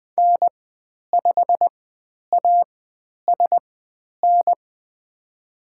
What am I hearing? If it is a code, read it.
Morse code, heard as a keyed tone, about 20 wpm, spaced as for 10 wpm, 700 Hz. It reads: N5ASN
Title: 095 Lesson 16g - Letter c - Call Signs 20wpm